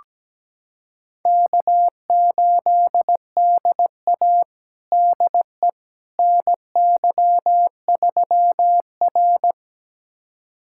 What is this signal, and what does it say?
Morse code, heard as a keyed tone, about 17 wpm, 700 Hz. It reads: K8DA DE NY3R
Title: USA Callsign DE Callsign - 4 Characters - 17wpm